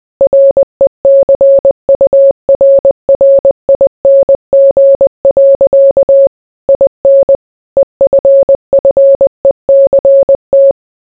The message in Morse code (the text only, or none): RECURRING. IN EFFECT